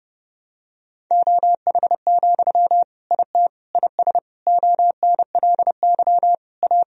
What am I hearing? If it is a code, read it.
Morse code, heard as a keyed tone, about 32 wpm, 700 Hz. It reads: OH, IT IS ONLY A
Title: Sentences from Top 1000 Words - Review Part 3 - 30wpm